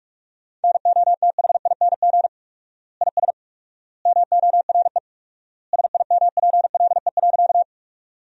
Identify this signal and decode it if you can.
Morse code, heard as a keyed tone, about 46 wpm, 700 Hz. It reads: NOTHING IS MORE SIMPLE.
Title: 4x Word-Spacing - Sentences from Top 500 Words - 45wpm